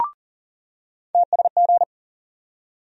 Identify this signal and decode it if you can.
Morse code, heard as a keyed tone, about 40 wpm, 700 Hz. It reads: TSG
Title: USA Call Sign Postfixes - 3 Letters 40wpm